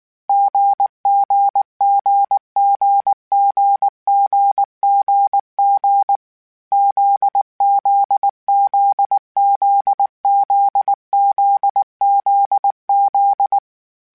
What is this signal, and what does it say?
Morse code, heard as a keyed tone, about 19 wpm, 800 Hz. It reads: GGGGGGGG ZZZZZZZZ